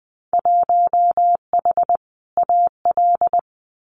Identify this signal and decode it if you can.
Morse code, heard as a keyed tone, about 20 wpm, 700 Hz. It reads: 1H AL
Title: Winter Field Day Exchanges 20wpm